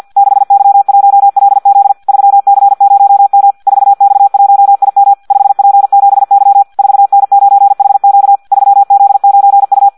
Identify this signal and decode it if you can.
Morse code, heard as a keyed tone, about 50 wpm, 800 Hz. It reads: 6Y1LZ 3C0M 4X1IM 5P7Y 4N9HQ 3Z9U
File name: Audio practica CW / qrq-tx-50 2